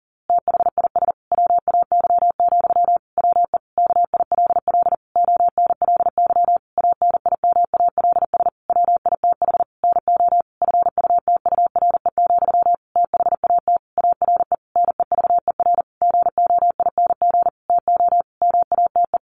Morse code, heard as a keyed tone, about 37 wpm, 700 Hz. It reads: THIS WAY, WE KILL ONLY ANIMALS WITH NO FUTURE, THAT ARE NEVER GOING TO MATE
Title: A Sound Of Thunder 40wpm